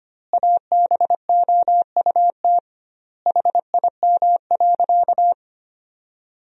Morse code, heard as a keyed tone, about 25 wpm, 700 Hz. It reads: ABOUT HIM.
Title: 2x Word-Spacing - Sentences from Top 200 Words - 25wpm